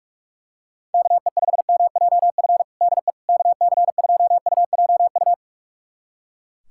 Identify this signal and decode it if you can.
Morse code, heard as a keyed tone, about 42 wpm, 700 Hz. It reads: KE5MJF DE KC2UJU
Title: USA Callsign DE Callsign - 6 Characters - 45wpm